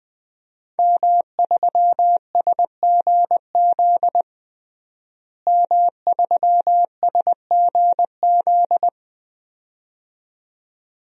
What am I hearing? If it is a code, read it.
Morse code, heard as a keyed tone, about 21 wpm, 700 Hz. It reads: M3SGZ M3SGZ
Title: DXCC-100 Call Signs - 5 Characters - Repeated 2x - 20wpm